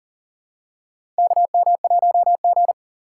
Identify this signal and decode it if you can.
Morse code, heard as a keyed tone, about 40 wpm, 700 Hz. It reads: KM1G
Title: USA Call Signs - 4 Characters - 3x Character Spacing - 40wpm